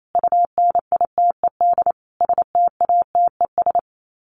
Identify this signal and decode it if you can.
Morse code, heard as a keyed tone, about 28 wpm, 700 Hz. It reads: UNITED STATES
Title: Sets of 5 Words Encore2 28wpm